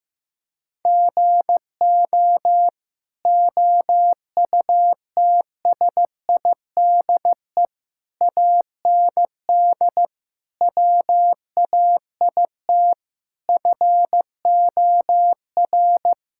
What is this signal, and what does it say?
Morse code, heard as a keyed tone, about 16 wpm, 700 Hz. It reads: GO OUTSIDE AND WAIT FOR
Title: Sentences from Top 1000 Words - Review Part 6 - 15wpm